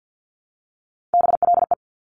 Morse code, heard as a keyed tone, about 50 wpm, 700 Hz. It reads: BLE
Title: Top Three Letter Combinations in Words 50wpm